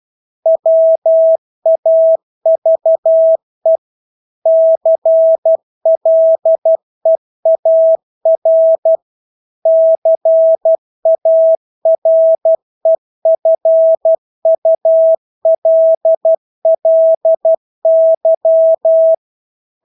Morse code, heard as a keyed tone, about 12 wpm, 650 Hz. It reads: WAVE CLEAR CAREFULLY
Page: Engelske ord 12 wpm